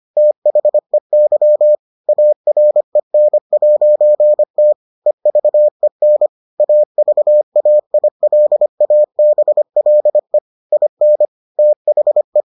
Morse code, heard as a keyed tone, about 25 wpm, 600 Hz. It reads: THEY AREN'T EVEN AVAILABLE IN THE